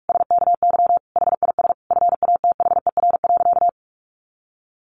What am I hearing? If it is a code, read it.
Morse code, heard as a keyed tone, about 46 wpm, 700 Hz. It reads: SKY HIS FATHER.
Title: Sentences from Top 1200 Words 45wpm